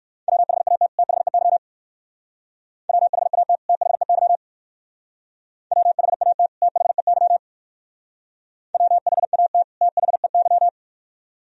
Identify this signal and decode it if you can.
Morse code, heard as a keyed tone, about 59 wpm, 690 Hz. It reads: WHAT THEY WHAT THEY WHAT THEY WHAT THEY